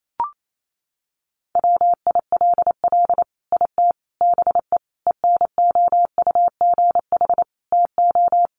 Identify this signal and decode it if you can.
Morse code, heard as a keyed tone, about 28 wpm, 700 Hz. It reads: WILL IT BE ENOUGH TO
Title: Sentences from Top 1100 Words 28wpm